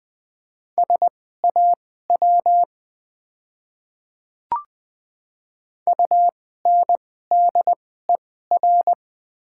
Morse code, heard as a keyed tone, about 20 wpm, spaced as for 14 wpm, 690 Hz. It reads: SAW UNDER
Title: 2x Character Spacing - Top 300 Words - 20wpm